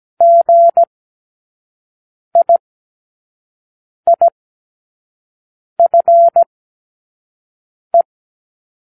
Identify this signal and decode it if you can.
Morse code, heard as a keyed tone, about 17 wpm, spaced as for 5 wpm, 680 Hz. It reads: GIIFE
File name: k7qo / Lesson15(A-J)